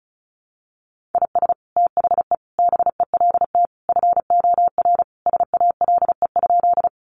Morse code, heard as a keyed tone, about 34 wpm, 700 Hz. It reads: IS THE BELT FOR SALE?
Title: Sentences from Top 1400 Words 35wpm